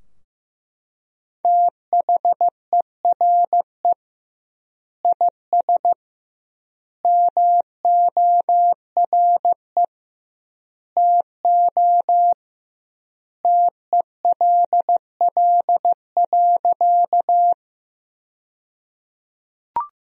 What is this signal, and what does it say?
Morse code, heard as a keyed tone, about 16 wpm, 700 Hz. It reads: THERE IS MORE TO TELL.
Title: 2x Word-Spacing - Sentences from Top 200 Words - 15wpm